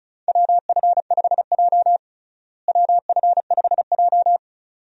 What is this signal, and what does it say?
Morse code, heard as a keyed tone, about 36 wpm, 700 Hz. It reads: WF5J WF5J
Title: USA Call Signs - 4 Characters - Repeated 2x - 35wpm